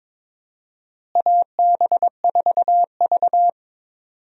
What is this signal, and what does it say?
Morse code, heard as a keyed tone, about 22 wpm, 700 Hz. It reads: AB4V
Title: USA Call Signs - 4 Characters - Repeated 2x - 22wpm